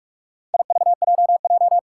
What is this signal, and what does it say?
Morse code, heard as a keyed tone, about 46 wpm, 700 Hz. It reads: IVJJ
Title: Four Challenging Letter-Number Combinations 45wpm